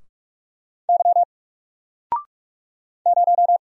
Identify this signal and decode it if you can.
Morse code, heard as a keyed tone, about 44 wpm, 700 Hz. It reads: Y 0
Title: Single Letter-Number Alt 45wpm